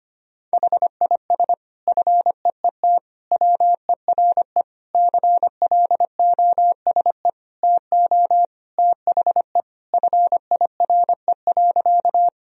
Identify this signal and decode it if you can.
Morse code, heard as a keyed tone, about 25 wpm, 700 Hz. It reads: HIS FEET WERE CLOSE TO THE FIRE.
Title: Sentences from Top 1000 Words - Review Part 1 - 25wpm